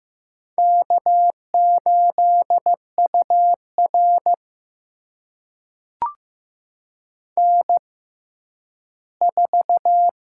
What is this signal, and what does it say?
Morse code, heard as a keyed tone, about 15 wpm, spaced as for 10 wpm, 700 Hz. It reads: K8UR N4
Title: USA Call Signs - 4 Characters - 6x Character Spacing - 15wpm